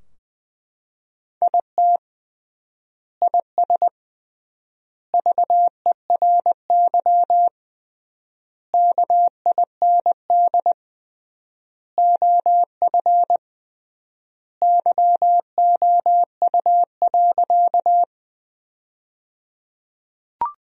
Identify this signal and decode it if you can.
Morse code, heard as a keyed tone, about 20 wpm, 700 Hz. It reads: IT IS VERY KIND OF YOU.
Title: 3x Word-Spacing - Sentences from Top 200 Words - 20wpm